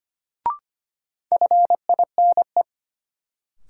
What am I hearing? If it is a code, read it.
Morse code, heard as a keyed tone, about 25 wpm, 700 Hz. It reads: FINE
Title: Sentences from Top 500 Words - New Word Drill - 25wpm